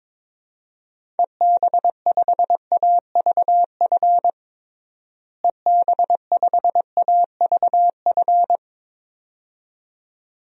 Morse code, heard as a keyed tone, about 22 wpm, 700 Hz. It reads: EB5AVF EB5AVF
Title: DXCC-100 Call Signs - 6 Characters - Repeated 2x - 22wpm